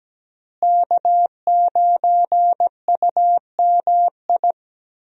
Morse code, heard as a keyed tone, about 17 wpm, 700 Hz. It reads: K9UMI
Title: USA Call Signs - 5 Characters - Repeated 3x - 17wpm